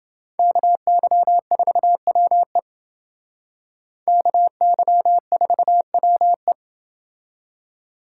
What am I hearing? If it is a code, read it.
Morse code, heard as a keyed tone, about 29 wpm, 700 Hz. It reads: KY4WE KY4WE